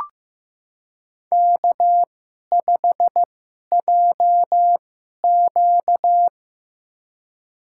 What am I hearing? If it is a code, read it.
Morse code, heard as a keyed tone, about 15 wpm, spaced as for 11 wpm, 700 Hz. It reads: K5JQ
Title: USA Call Signs - 4 Characters - 2x Character Spacing - 15wpm